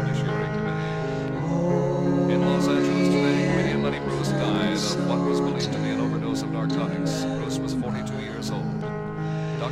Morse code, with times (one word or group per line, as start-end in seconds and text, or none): none